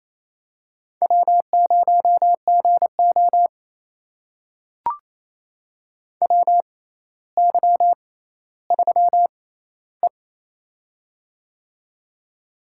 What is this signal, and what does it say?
Morse code, heard as a keyed tone, about 29 wpm, spaced as for 15 wpm, 700 Hz. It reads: W0GO WY3E
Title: USA Call Signs - 4 Characters - 6x Character Spacing - 28wpm